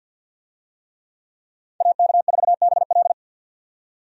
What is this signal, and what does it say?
Morse code, heard as a keyed tone, about 52 wpm, 700 Hz. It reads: AK4BL
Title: USA Call Signs - 5 Characters - 50wpm